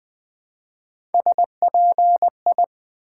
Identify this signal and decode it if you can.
Morse code, heard as a keyed tone, about 20 wpm, 700 Hz. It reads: SPI
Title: Build A Word - 1 Letter At A Time - 5-Letter Words - 20wpm